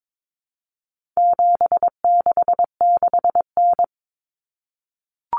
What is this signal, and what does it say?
Morse code, heard as a keyed tone, about 22 wpm, 700 Hz. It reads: 766N
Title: Common Cut Numbers - 4-Digit 22wpm